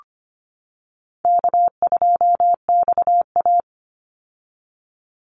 Morse code, heard as a keyed tone, about 25 wpm, 700 Hz. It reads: K2XA
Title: Easy Call Signs - 25wpm